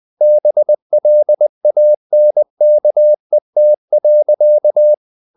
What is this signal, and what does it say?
Morse code, heard as a keyed tone, about 20 wpm, 600 Hz. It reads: BLANKET.
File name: the old man and the sea / the old man and the sea part 1 20wpm